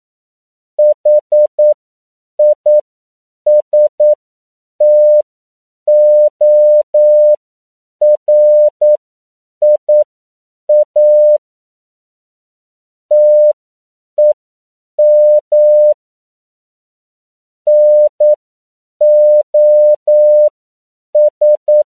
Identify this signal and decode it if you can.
Morse code, heard as a keyed tone, about 9 wpm, spaced as for 7 wpm, 600 Hz. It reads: HISTORIA TEM NOS